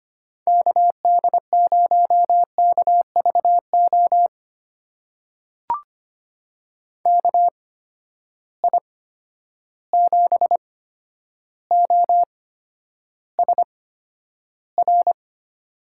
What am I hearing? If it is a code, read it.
Morse code, heard as a keyed tone, about 25 wpm, spaced as for 11 wpm, 700 Hz. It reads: KD0KVO KI7OSR
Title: USA Call Signs - 6 Characters - 8x Character Spacing - 25wpm